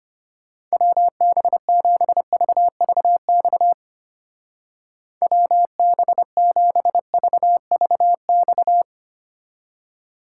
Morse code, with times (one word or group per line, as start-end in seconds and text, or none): 0.7-3.7 WB7VVX
5.2-8.8 WB7VVX